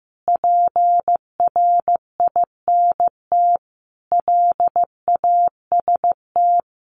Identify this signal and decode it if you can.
Morse code, heard as a keyed tone, about 14 wpm, 700 Hz. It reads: PRINT LAST
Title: Sentences from Top 1400 Words 15wpm